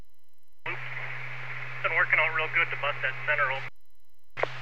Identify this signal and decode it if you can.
Emergency Alert System (SAME) tones, heard as harsh buzzing bursts, with speech. It is heard from 0.8 to 3.5 s.